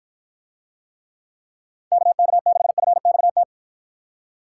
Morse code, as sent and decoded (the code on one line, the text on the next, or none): -.- -.- -.... ..-. -..- -
KK6FXT